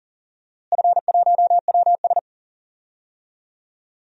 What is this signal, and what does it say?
Morse code, heard as a keyed tone, about 40 wpm, 700 Hz. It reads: F1WS
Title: DXCC-100 Call Signs - 4 Characters - Repeated 3x - 40wpm